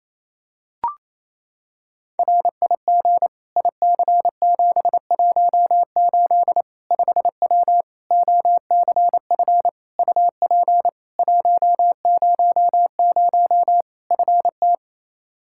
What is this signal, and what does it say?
Morse code, heard as a keyed tone, about 28 wpm, 700 Hz. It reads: RIG IC718 5W OCF UP 100 FT
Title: K1USN SST QSO Practice 28wpm